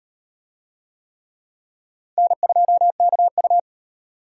Morse code, heard as a keyed tone, about 34 wpm, 700 Hz. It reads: N2KU